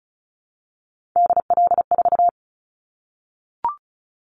Morse code, heard as a keyed tone, about 34 wpm, 700 Hz. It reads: DL4
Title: Callsign Prefixes for Top 50 DXCC - 35wpm